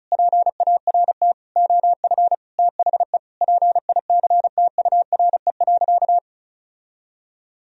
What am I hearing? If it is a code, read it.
Morse code, heard as a keyed tone, about 36 wpm, 700 Hz. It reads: PART OF THE PICTURE.
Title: Sentences from Top 1000 Words - Review Part 4 - 35wpm